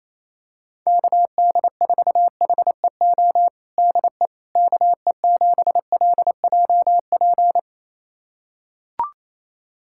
Morse code, heard as a keyed tone, about 28 wpm, 700 Hz. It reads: KD4HEO DE KE7LJP
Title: USA Callsign DE Callsign 28wpm